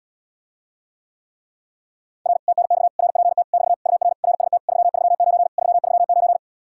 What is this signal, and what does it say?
Morse code, heard as a keyed tone, about 80 wpm, 700 Hz. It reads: R TNX RPT <BT> UR RST 339 339